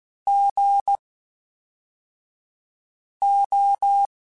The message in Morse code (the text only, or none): GO